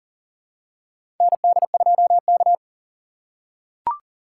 Morse code, heard as a keyed tone, about 40 wpm, 690 Hz. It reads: ND2K
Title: USA Call Signs - 2x Character Spacing - 40wpm